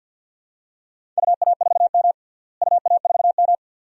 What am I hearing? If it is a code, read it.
Morse code, heard as a keyed tone, about 52 wpm, 700 Hz. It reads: UA4M UA4M
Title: DXCC-100 Call Signs - 4 Characters - Repeated 2x - 50wpm